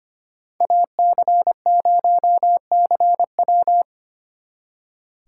Morse code, heard as a keyed tone, about 32 wpm, 700 Hz. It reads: AC0CW